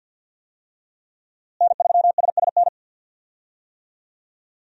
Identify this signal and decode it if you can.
Morse code, heard as a keyed tone, about 48 wpm, 700 Hz. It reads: N3SSN
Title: USA Call Signs - 5 Characters - Repeated 3x - 50wpm